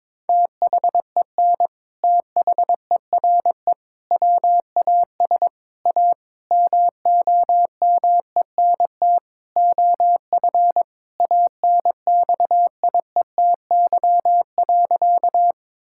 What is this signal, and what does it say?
Morse code, heard as a keyed tone, about 22 wpm, 700 Hz. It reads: THEN THERE WAS A MOMENT OF ANXIETY.
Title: Sentences from Top 1600 Words 22wpm